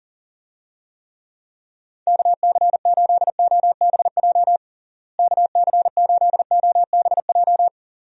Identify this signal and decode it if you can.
Morse code, heard as a keyed tone, about 40 wpm, 700 Hz. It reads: KC8OBJ KC8OBJ
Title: USA Call Signs - 6 Characters - Repeated 2x - 40wpm